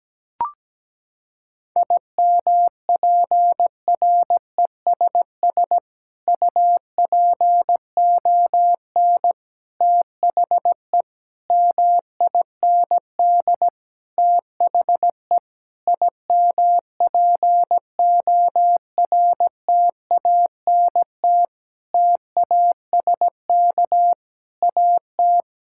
Morse code, heard as a keyed tone, about 17 wpm, 700 Hz. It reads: IMPRESS UPON THE MIND THE IMPORTANT TASK AT
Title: Sentences from Top 1600 Words 17wpm